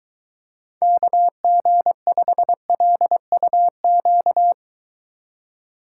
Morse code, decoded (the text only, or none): KG5LUQ